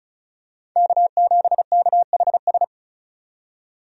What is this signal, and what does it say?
Morse code, heard as a keyed tone, about 36 wpm, 700 Hz. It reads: K7KHS
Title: USA Call Signs - 5 Characters - 35wpm